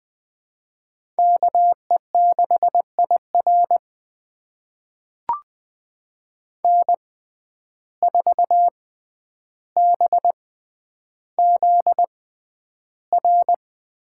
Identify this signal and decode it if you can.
Morse code, heard as a keyed tone, about 21 wpm, spaced as for 10 wpm, 700 Hz. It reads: KE6IR N4BZR
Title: USA Call Signs - 5 Characters - 6x Character Spacing - 20wpm